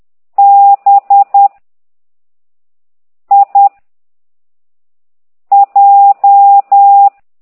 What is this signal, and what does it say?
Morse code, heard as a keyed tone, about 10 wpm, 800 Hz. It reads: B I J
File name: exerc7